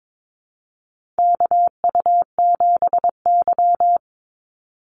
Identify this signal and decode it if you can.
Morse code, heard as a keyed tone, about 22 wpm, 700 Hz. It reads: KU7Y
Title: Call Signs Repeated 2x - 22wpm